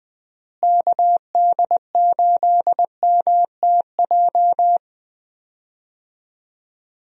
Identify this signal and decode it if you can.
Morse code, heard as a keyed tone, about 20 wpm, 700 Hz. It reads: KD8MTJ
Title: USA Call Signs - 6 Characters - Repeated 3x - 20wpm